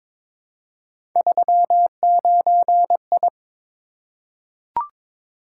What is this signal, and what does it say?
Morse code, heard as a keyed tone, about 22 wpm, 700 Hz. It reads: 39I
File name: practice / 22wpm / Three Letter-Number Combinations - 22wpm